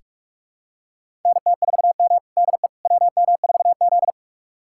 Morse code, heard as a keyed tone, about 46 wpm, 700 Hz. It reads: NT4M DE WM4Z
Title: USA Callsign DE Callsign - 4 Characters - 45wpm